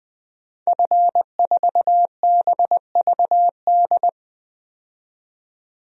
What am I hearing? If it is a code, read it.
Morse code, heard as a keyed tone, about 20 wpm, 700 Hz. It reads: F4BVD
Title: DXCC-100 Call Signs - 5 Characters - 20wpm